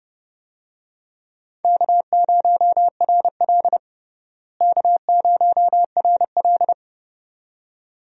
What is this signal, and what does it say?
Morse code, heard as a keyed tone, about 32 wpm, 700 Hz. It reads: K0RL K0RL
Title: USA Call Signs - 4 Characters - Repeated 2x - 30wpm